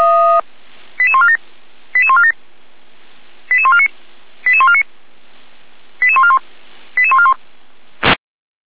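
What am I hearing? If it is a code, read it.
Five-tone selective call (ZVEI), heard as quick runs of five tones, tones of about 70 ms. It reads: 88147 88147 88149 88149 88141 88141
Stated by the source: Fire brigade ZVEI selective call received by SDR, Feuerwehr Orlamünde.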